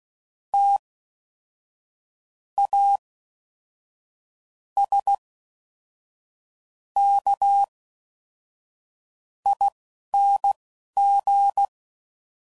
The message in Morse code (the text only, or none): TASKING